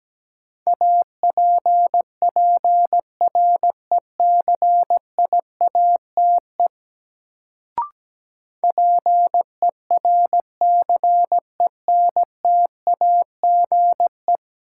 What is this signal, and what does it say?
Morse code, heard as a keyed tone, about 17 wpm, 700 Hz. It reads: APPRECIATE PERCENTAGE
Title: Sentences from Top 1300 Words - New Word Drill - 17wpm